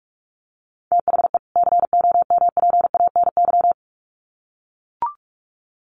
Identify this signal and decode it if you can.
Morse code, heard as a keyed tone, about 44 wpm, 700 Hz. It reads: THE COMPANY
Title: Sets of 2 Words 45wpm